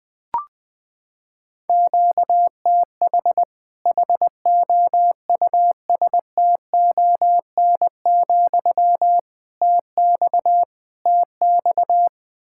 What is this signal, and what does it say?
Morse code, heard as a keyed tone, about 21 wpm, 700 Hz. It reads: QTH HOUSTON, TX TX
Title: Practice QSO - Exchange 1 - Beginning - 20wpm